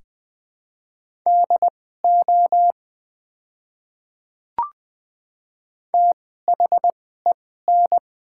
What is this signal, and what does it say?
Morse code, heard as a keyed tone, about 20 wpm, spaced as for 14 wpm, 700 Hz. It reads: DO THEN
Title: 2x Character Spacing - Top 200 Words - 20wpm